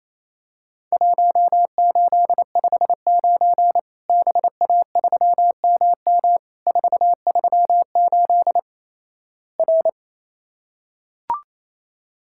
Morse code, heard as a keyed tone, about 27 wpm, 700 Hz. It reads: 1859 BA3MM 438 R
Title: Worked All Europe DX Contest - QTC Exchange - 28wpm